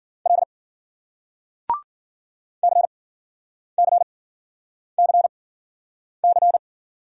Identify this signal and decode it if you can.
Morse code, heard as a keyed tone, about 54 wpm, spaced as for 9 wpm, 700 Hz. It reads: L CCCC